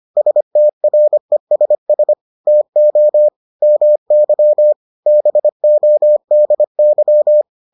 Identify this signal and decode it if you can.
Morse code, heard as a keyed tone, about 25 wpm, 600 Hz. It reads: STRESS TO MY BODY